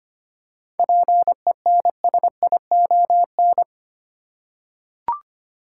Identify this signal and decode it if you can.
Morse code, heard as a keyed tone, about 25 wpm, 700 Hz. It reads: PENSION
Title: Seven Letter Words - 25wpm